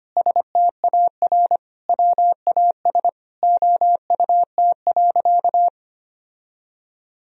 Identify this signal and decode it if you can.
Morse code, heard as a keyed tone, about 25 wpm, 700 Hz. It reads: STAR WAS OUT.
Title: Sentences from Top 1000 Words - Review Part 4 - 25wpm